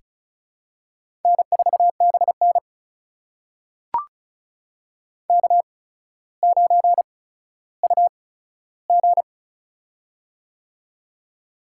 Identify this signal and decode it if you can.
Morse code, heard as a keyed tone, about 36 wpm, spaced as for 15 wpm, 700 Hz. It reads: N4BN K9UG
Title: USA Call Signs - 4 Characters - 8x Character Spacing - 35wpm